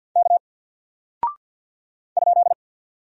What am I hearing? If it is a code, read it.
Morse code, heard as a keyed tone, about 48 wpm, 700 Hz. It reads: K ?
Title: Single Character (Letter:Number:Punctuation:Prosign) Alt 50wpm